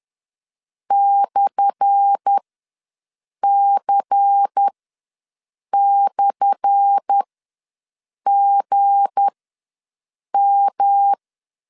Morse code, heard as a keyed tone, about 10 wpm, spaced as for 6 wpm, 800 Hz. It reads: /C/GM
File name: LEZIONI / 10 SIGNS / 10A SLASH ws 10wpm 4121